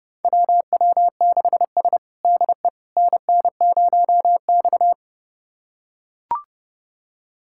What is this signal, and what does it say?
Morse code, heard as a keyed tone, about 30 wpm, 700 Hz. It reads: WW6S DE NN0X
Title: USA Callsign DE Callsign - 4 Characters - 30wpm